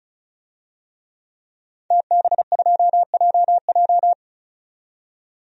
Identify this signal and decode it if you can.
Morse code, heard as a keyed tone, about 34 wpm, 700 Hz. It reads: TB2JJ
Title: DXCC-100 Call Signs - 5 Characters - 35wpm